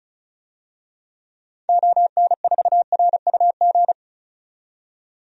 Morse code, heard as a keyed tone, about 36 wpm, 700 Hz. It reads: ON4RUG
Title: DXCC-100 Call Signs - 6 Characters - 35wpm